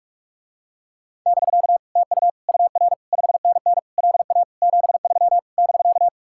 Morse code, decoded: <BK> TU UR 5NN LA 73 <BK>

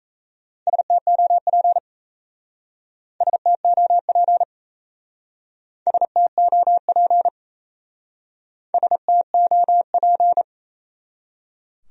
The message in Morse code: STOP STOP STOP STOP

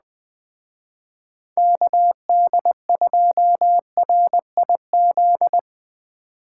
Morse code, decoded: KD2RIZ

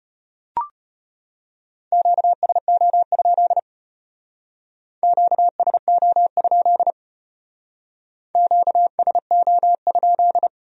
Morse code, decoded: QSO? QSO? QSO?

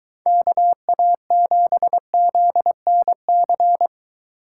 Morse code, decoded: KA7ZNC

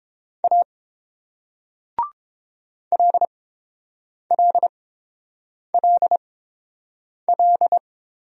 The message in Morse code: A LLLL

